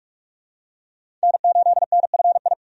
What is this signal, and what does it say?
Morse code, heard as a keyed tone, about 44 wpm, 700 Hz. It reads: N8NFI
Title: USA Call Signs - 5 Characters - 45wpm